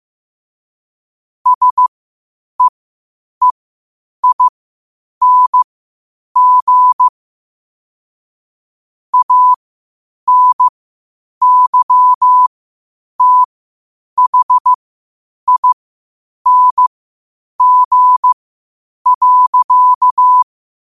Morse code, heard as a keyed tone, about 15 wpm, spaced as for 8 wpm, 1000 Hz. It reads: SEEING ANYTHING.
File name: qotdfiles / 8wpm